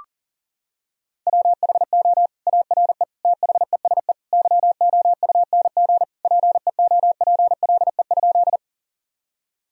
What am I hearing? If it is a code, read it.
Morse code, heard as a keyed tone, about 37 wpm, 700 Hz. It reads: WHO ARE THESE YOUNG PEOPLE?
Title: Sentences from Top 1000 Words - Review Part 1 - 40wpm